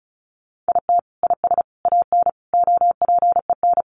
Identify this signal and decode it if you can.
Morse code, heard as a keyed tone, about 34 wpm, 700 Hz. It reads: IT IS AN OPEN
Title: Sentences from Top 1300 Words 35wpm